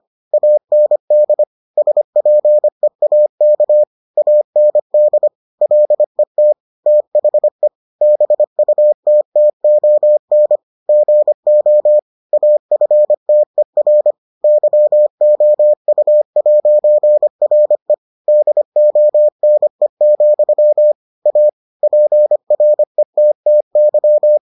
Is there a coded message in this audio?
Yes: Morse code from 0.3 to 24.4 s, heard as a keyed tone, about 25 wpm, 600 Hz. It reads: AND SPEAK AND LET THE BUTTON GO AFTER YOU'RE DONE, A PRETTY